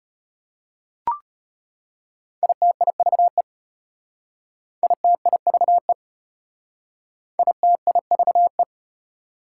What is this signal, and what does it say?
Morse code, heard as a keyed tone, about 32 wpm, 690 Hz. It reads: ITIVE ITIVE ITIVE